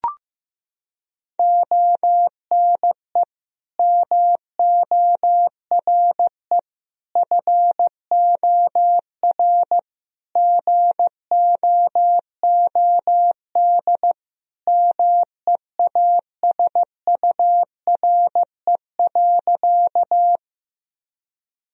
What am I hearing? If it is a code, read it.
Morse code, heard as a keyed tone, about 15 wpm, 700 Hz. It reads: ONE MORE FOR GOOD MEASURE.
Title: Sentences from Top 1000 Words - Review Part 2 - 15wpm